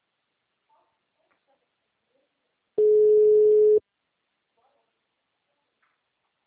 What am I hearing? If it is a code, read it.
Ringback tone (Europe, 425 Hz).